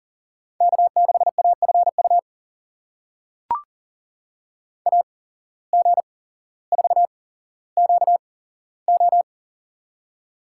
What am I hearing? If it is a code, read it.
Morse code, heard as a keyed tone, about 40 wpm, spaced as for 17 wpm, 700 Hz. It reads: K6AFU AG4QO
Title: USA Call Signs - 5 Characters - 8x Character Spacing - 40wpm